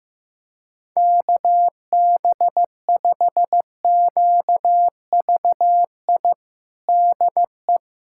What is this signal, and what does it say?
Morse code, heard as a keyed tone, about 16 wpm, 700 Hz. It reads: KB5QVI DE
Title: USA Callsign DE Callsign - 6 Characters - 15wpm